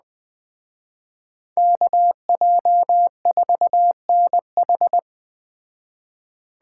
Morse code, heard as a keyed tone, about 20 wpm, 700 Hz. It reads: KJ4NH
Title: Speed Racing - USA 2x2 Call Signs 17wpm